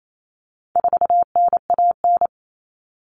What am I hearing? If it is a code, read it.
Morse code, heard as a keyed tone, about 28 wpm, 700 Hz. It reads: 4NAN
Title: Common Cut Numbers - 4-Digit 28wpm